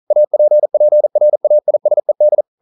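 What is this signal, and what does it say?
Morse code, heard as a keyed tone, about 42 wpm, 600 Hz. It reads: APPRAISED
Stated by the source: Morse code practice at 41 wpm, from CW med Gnister.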